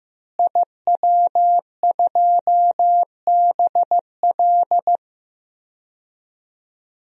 Morse code, heard as a keyed tone, about 15 wpm, 700 Hz. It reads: IW2BL
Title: DXCC-100 Call Signs - 5 Characters - Repeated 3x - 15wpm